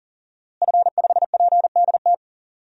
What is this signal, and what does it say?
Morse code, heard as a keyed tone, about 40 wpm, 700 Hz. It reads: F5PDT
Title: DXCC-100 Call Signs - 5 Characters - Repeated 2x - 40wpm